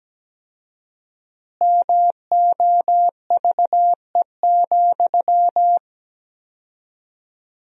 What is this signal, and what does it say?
Morse code, heard as a keyed tone, about 17 wpm, 700 Hz. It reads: MOVE,
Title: 4x Word-Spacing - Sentences from Top 300 Words - 17wpm